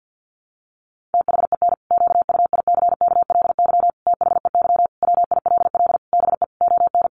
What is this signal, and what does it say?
Morse code, heard as a keyed tone, about 48 wpm, 700 Hz. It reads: THEN QUICKLY THEY WILL BE ON
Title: Sentences from Top 700 Words 50wpm